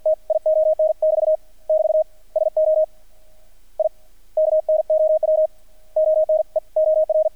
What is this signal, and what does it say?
Morse code, heard as a keyed tone, about 48 wpm, 630 Hz. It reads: TION <BT> <BT> SO I KNOW ONE OR
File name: archived / CRSnet